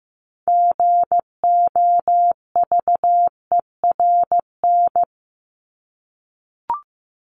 Morse code, heard as a keyed tone, about 15 wpm, 700 Hz. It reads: GOVERN
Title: Top 500 Words Repeated 15wpm